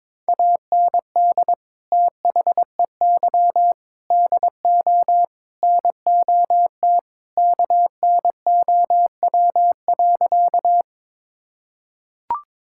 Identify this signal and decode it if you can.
Morse code, heard as a keyed tone, about 21 wpm, 700 Hz. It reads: AND THEY DO NOT KNOW.
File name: practice / 22wpm / Sentences with 4-Word Phrases - 22wpm